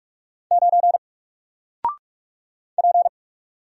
Morse code, heard as a keyed tone, about 44 wpm, 700 Hz. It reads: 9 P
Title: Single Character (Letter:Number:Punctuation:Prosign) 45wpm